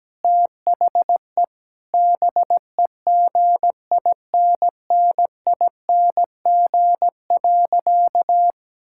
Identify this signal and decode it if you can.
Morse code, heard as a keyed tone, about 17 wpm, 700 Hz. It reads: THE BEGINNING.
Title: Sentences from Top 1000 Words - Review Part 5 - 17wpm